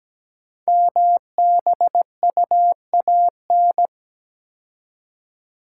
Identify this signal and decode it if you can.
Morse code, heard as a keyed tone, about 17 wpm, 700 Hz. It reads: MBUAN
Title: Code Groups - 5 Letters - 17wpm